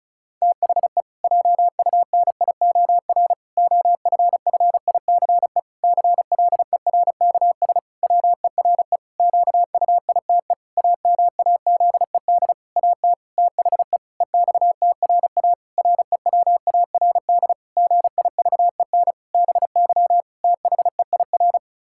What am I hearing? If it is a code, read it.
Morse code, heard as a keyed tone, about 36 wpm, 700 Hz. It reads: THE JUNIOR OFFICE CLERKS WERE QUITE AMAZED AT THE EXTRA REWARD GIVEN BY THEIR
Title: Pangram Challenge v2 - 35wpm